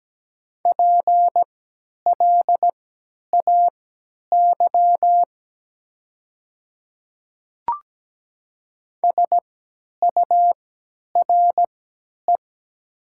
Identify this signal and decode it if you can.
Morse code, heard as a keyed tone, about 17 wpm, spaced as for 10 wpm, 700 Hz. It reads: PLAY SURE
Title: 3x Character Spacing - Top 300 Words - 17wpm